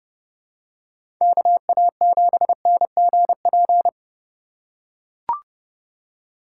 Morse code, heard as a keyed tone, about 29 wpm, 700 Hz. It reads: KA7NGP